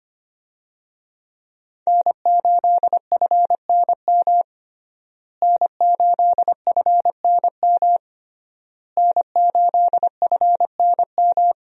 Morse code, heard as a keyed tone, about 25 wpm, 700 Hz. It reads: N8FNM N8FNM N8FNM
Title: USA Call Signs - 5 Characters - Repeated 3x - 25wpm